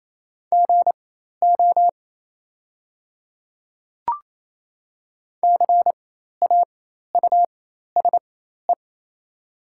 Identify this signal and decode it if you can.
Morse code, heard as a keyed tone, about 28 wpm, spaced as for 13 wpm, 700 Hz. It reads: GO CAUSE
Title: 4x Character Spacing - Top 200 Words - 28wpm